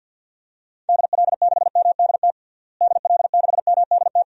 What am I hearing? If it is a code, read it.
Morse code, heard as a keyed tone, about 53 wpm, 700 Hz. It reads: DL6MDT DL6MDT
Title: DXCC-100 Call Signs - 6 Characters - Repeated 2x - 50wpm